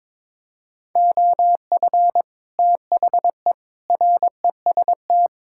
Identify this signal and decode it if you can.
Morse code, heard as a keyed tone, about 21 wpm, 700 Hz. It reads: OF THE REST
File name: practice / 22wpm / Build A Phrase - 5-Word Phrases - 22wpm